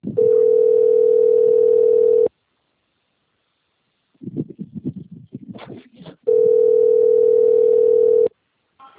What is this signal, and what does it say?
Ringback tone (North America), 440 + 480 Hz.